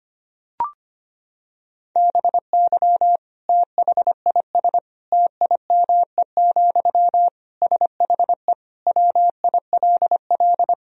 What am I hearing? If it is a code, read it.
Morse code, heard as a keyed tone, about 25 wpm, 700 Hz. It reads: BY THIS TIME, SHE WILL
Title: Sentences from Top 1000 Words - Review Part 4 - 25wpm